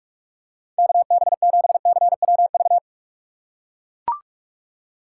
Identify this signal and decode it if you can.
Morse code, heard as a keyed tone, about 44 wpm, 700 Hz. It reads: KB7CWV
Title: USA Call Signs - 6 Characters - 6x Character Spacing - 45wpm